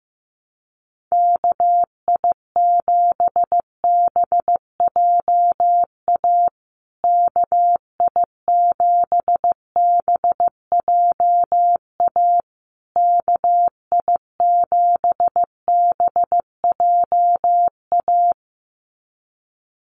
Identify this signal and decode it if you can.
Morse code, heard as a keyed tone, about 15 wpm, 700 Hz. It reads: KI7BJA KI7BJA KI7BJA
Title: USA Call Signs Repeated 3x 15wpm